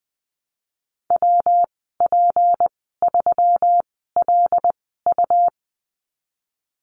Morse code, heard as a keyed tone, about 20 wpm, spaced as for 15 wpm, 700 Hz. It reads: WP3LU
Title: 154 Lesson 25g - Letter p - Call Signs 20wpm